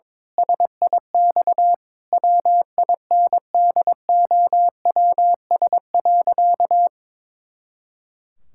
Morse code, heard as a keyed tone, about 22 wpm, 700 Hz. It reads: SIX WINDOWS.
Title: Pangram Challenge v2 - 22wpm